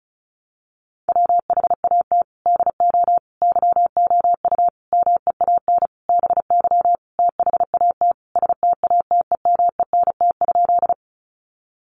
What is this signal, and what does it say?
Morse code, heard as a keyed tone, about 34 wpm, 700 Hz. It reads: WHAT DO YOU MEAN BY THAT STATEMENT?
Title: Sentences from Top 1200 Words 35wpm